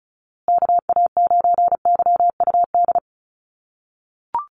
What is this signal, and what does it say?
Morse code, heard as a keyed tone, about 34 wpm, 700 Hz. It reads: KA9YUD
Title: USA Call Signs Repeated 3x 35wpm